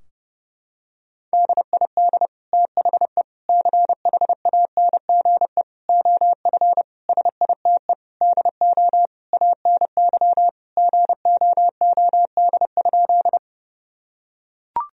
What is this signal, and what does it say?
Morse code, heard as a keyed tone, about 30 wpm, 700 Hz. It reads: DID THE CHANGE OF SITE DO ANY GOOD?
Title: Sentences from Top 1000 Words - Review Part 4 - 30wpm